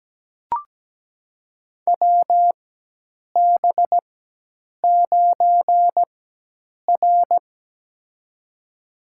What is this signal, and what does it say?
Morse code, heard as a keyed tone, about 17 wpm, spaced as for 8 wpm, 700 Hz. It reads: WB9R
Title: USA Call Signs - 4 Characters - 4x Character Spacing - 17wpm